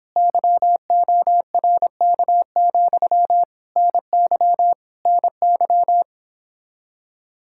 Morse code, heard as a keyed tone, about 26 wpm, 700 Hz. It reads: YORK, NY NY